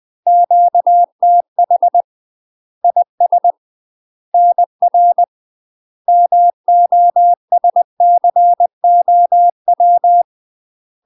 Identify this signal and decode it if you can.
Morse code, heard as a keyed tone, about 20 wpm, 700 Hz. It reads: QTH IS NR MOSCOW
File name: UK MASTER QSOs TRAINING / 20WPM / UK MASTER QSOs 0004